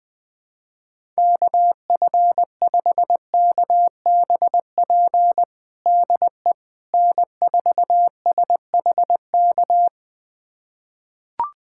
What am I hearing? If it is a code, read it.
Morse code, heard as a keyed tone, about 20 wpm, 700 Hz. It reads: KF5KBP DE N4SHK
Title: USA Callsign DE Callsign 20wpm